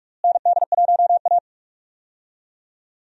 Morse code, heard as a keyed tone, about 46 wpm, 700 Hz. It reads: ND1A
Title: USA Call Signs - 4 Characters - Repeated 3x - 45wpm